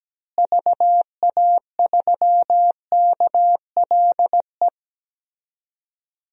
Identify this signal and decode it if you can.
Morse code, heard as a keyed tone, about 17 wpm, 700 Hz. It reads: VA3KLE
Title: DXCC-100 Call Signs - 6 Characters - Repeated 3x - 17wpm